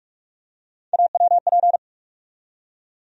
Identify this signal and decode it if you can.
Morse code, heard as a keyed tone, about 46 wpm, 700 Hz. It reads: AWP
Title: USA Call Sign Postfixes - 3 Letters 45wpm